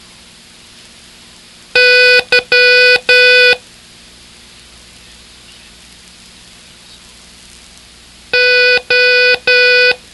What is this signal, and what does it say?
Morse code, heard as a keyed tone, about 12 wpm, 1490 Hz. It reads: Y O